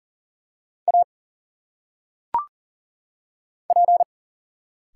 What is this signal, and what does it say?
Morse code, heard as a keyed tone, about 40 wpm, 700 Hz. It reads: A P
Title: Single Letters - Phonetic Pronunciation 40wpm